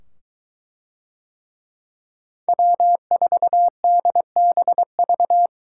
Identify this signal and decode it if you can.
Morse code, heard as a keyed tone, about 22 wpm, 700 Hz. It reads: W4DBV